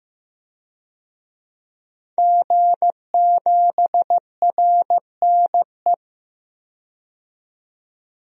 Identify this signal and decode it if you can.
Morse code, heard as a keyed tone, about 15 wpm, 700 Hz. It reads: G7RNE
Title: DXCC-100 Call Signs - 5 Characters - 15wpm